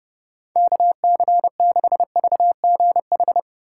Morse code, heard as a keyed tone, about 28 wpm, 700 Hz. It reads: KC6VGH